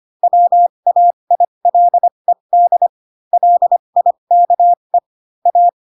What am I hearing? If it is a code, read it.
Morse code, heard as a keyed tone, about 25 wpm, spaced as for 22 wpm, 700 Hz. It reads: WAILED LIKE A
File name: Pinocchio / Pinocchio ch3 18 25 ik1hge